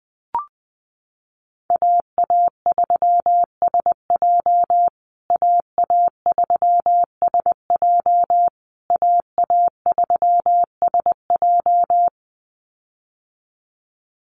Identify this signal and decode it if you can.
Morse code, heard as a keyed tone, about 20 wpm, 700 Hz. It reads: AA3SJ AA3SJ AA3SJ
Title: USA Call Signs Repeated 3x 20wpm